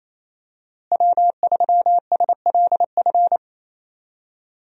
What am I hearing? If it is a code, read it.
Morse code, heard as a keyed tone, about 28 wpm, 700 Hz. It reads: W3SLF
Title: USA Call Signs - 5 Characters - Repeated 3x - 28wpm